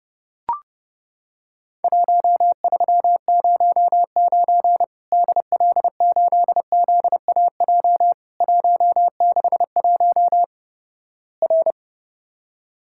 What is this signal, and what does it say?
Morse code, heard as a keyed tone, about 32 wpm, 700 Hz. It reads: 1309 DL8ZAJ 161 R
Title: Worked All Europe DX Contest - QTC Exchange - 30wpm